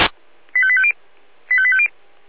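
Five-tone selective call (ZVEI), heard as quick runs of five tones, tones of about 70 ms. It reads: 85850 85850
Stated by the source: ZVEI five-tone selective call 25348 received by SDR.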